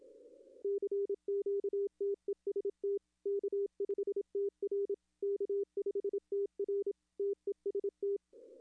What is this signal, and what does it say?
Morse code, heard as a keyed tone, about 27 wpm, 400 Hz. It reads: CQTEST K5TR K5TR TEST
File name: contests / arrl-160-k5tr-op-w5tn -1300